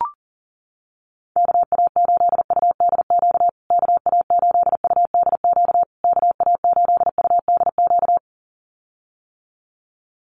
Morse code, heard as a keyed tone, about 40 wpm, 700 Hz. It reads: KA8UDQ KA8UDQ KA8UDQ
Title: USA Call Signs Repeated 3x 40wpm